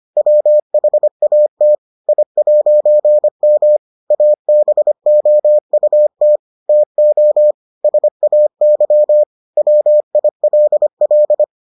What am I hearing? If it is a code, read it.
Morse code, heard as a keyed tone, about 25 wpm, 600 Hz. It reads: WHAT I'M ABOUT TO SAY WILL